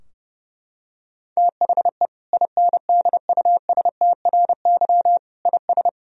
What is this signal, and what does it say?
Morse code, heard as a keyed tone, about 30 wpm, 700 Hz. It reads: THE INDUSTRY IS
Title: Sentences from Top 1000 Words - Review Part 4 - 30wpm